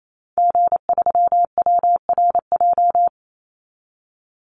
Morse code, heard as a keyed tone, about 28 wpm, 700 Hz. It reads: G3WRJ
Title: Call Signs - 28wpm